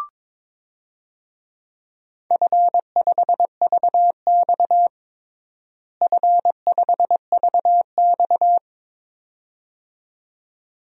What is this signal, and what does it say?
Morse code, heard as a keyed tone, about 22 wpm, 700 Hz. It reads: F5VX F5VX
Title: DXCC-100 Call Signs - 4 Characters - Repeated 2x - 22wpm